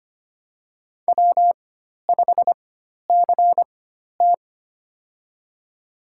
Morse code, heard as a keyed tone, about 25 wpm, spaced as for 12 wpm, 700 Hz. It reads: W5CT
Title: USA Call Signs - 4 Characters - 4x Character Spacing - 25wpm